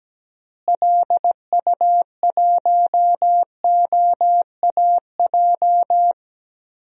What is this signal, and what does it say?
Morse code, heard as a keyed tone, about 17 wpm, 700 Hz. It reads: LU1OAJ
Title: DXCC-100 Call Signs - 6 Characters - Repeated 2x - 17wpm